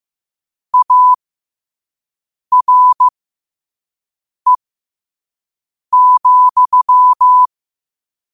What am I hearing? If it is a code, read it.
Morse code, heard as a keyed tone, about 15 wpm, spaced as for 5 wpm, 1000 Hz. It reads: ARE,
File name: qotdfiles / 5wpm